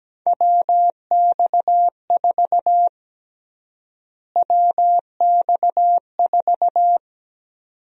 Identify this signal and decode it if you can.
Morse code, heard as a keyed tone, about 17 wpm, 700 Hz. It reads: WX4 WX4
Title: USA Call Sign Prefixes - Repeated 2x - 17wpm